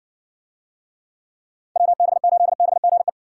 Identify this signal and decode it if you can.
Morse code, heard as a keyed tone, about 58 wpm, 700 Hz. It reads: WB8BGE